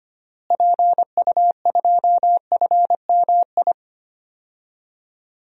Morse code, heard as a keyed tone, about 25 wpm, 700 Hz. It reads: PU2FMI
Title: DXCC-100 Call Signs - 6 Characters - Repeated 3x - 25wpm